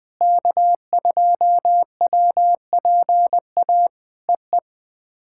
Morse code, heard as a keyed tone, about 20 wpm, 700 Hz. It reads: K2WPA EE
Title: POTA QSO Practice - 20wpm